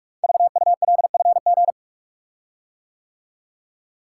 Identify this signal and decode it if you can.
Morse code, heard as a keyed tone, about 44 wpm, 700 Hz. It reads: VULFG
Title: Code Groups - 5 Letters - 45wpm